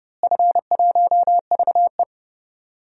Morse code, heard as a keyed tone, about 30 wpm, 700 Hz. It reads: F1VE
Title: DXCC-100 Call Signs - 4 Characters - Repeated 2x - 30wpm